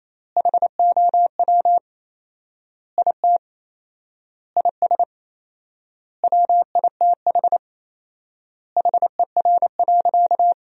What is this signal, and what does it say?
Morse code, heard as a keyed tone, about 27 wpm, 700 Hz. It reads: HOW IT IS WITH HER.
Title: 4x Word-Spacing - Sentences from Top 100 Words - 28wpm